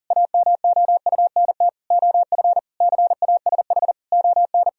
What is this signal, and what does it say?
Morse code, heard as a keyed tone, about 40 wpm, 700 Hz. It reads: AMOUNT OF CASH ON
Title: Sentences from Top 1000 Words - Review Part 4 - 40wpm